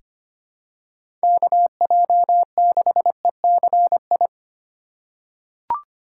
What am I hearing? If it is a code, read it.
Morse code, heard as a keyed tone, about 25 wpm, 700 Hz. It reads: KJ6ECI
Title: USA Call Signs - 6 Characters - 3x Character Spacing - 25wpm